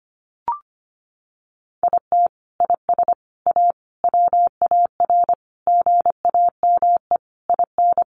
Morse code, heard as a keyed tone, about 25 wpm, 700 Hz. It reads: IT IS A WAR GAME IN
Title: Sentences from Top 1500 Words 25wpm